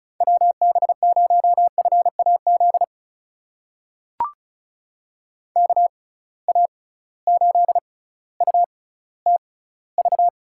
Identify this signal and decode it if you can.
Morse code, heard as a keyed tone, about 36 wpm, spaced as for 18 wpm, 700 Hz. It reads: WB0FAZ KA8UTV
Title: USA Call Signs - 6 Characters - 6x Character Spacing - 35wpm